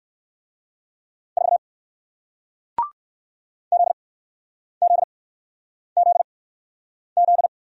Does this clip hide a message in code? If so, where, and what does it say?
Yes: Morse code from 1.4 to 7.5 s, heard as a keyed tone, about 60 wpm, spaced as for 9 wpm, 700 Hz. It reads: 4 ZZZZ